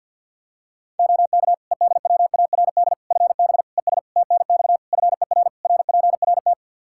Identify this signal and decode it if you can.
Morse code, heard as a keyed tone, about 46 wpm, 700 Hz. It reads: OK EDWARD FB ES TNX FER RPRT
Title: Practice QSO - Exchange 2 - Info - 50wpm